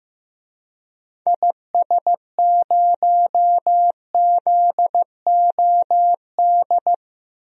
Morse code, heard as a keyed tone, about 15 wpm, 700 Hz. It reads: IS0ZOD
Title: DXCC-100 Call Signs - 6 Characters - 15wpm